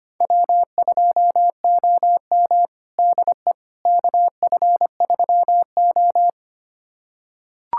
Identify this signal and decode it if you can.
Morse code, heard as a keyed tone, about 25 wpm, 700 Hz. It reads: W2OM DE KF3O
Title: USA Callsign DE Callsign - 4 Characters - 25wpm